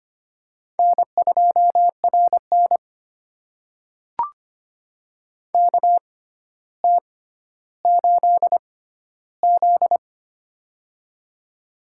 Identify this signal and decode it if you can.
Morse code, heard as a keyed tone, about 25 wpm, spaced as for 13 wpm, 700 Hz. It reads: N2RN KT8Z
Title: USA Call Signs - 4 Characters - 6x Character Spacing - 25wpm